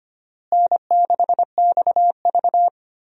Morse code, heard as a keyed tone, about 25 wpm, 700 Hz. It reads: N6XV
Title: USA Call Signs - 4 Characters - Repeated 3x - 25wpm